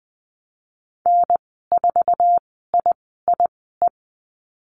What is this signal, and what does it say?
Morse code, heard as a keyed tone, about 19 wpm, spaced as for 14 wpm, 700 Hz. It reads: N4IIE
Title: 069 Lesson 12g - Letter d - Call Signs 20wpm